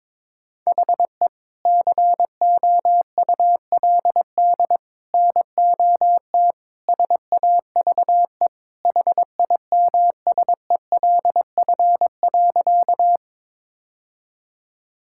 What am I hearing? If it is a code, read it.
Morse code, heard as a keyed tone, about 22 wpm, 700 Hz. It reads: HE COULD NOT SAVE HIMSELF.
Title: Sentences from Top 1600 Words 22wpm